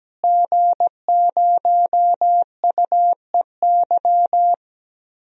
Morse code, heard as a keyed tone, about 17 wpm, 700 Hz. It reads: G0UEY
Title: DXCC-100 Call Signs - 5 Characters - Repeated 2x - 17wpm